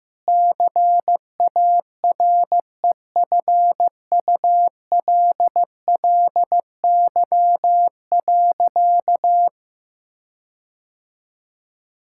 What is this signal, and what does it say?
Morse code, heard as a keyed tone, about 15 wpm, 700 Hz. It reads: CAREFULLY.
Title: Sentences from Top 1600 Words 15wpm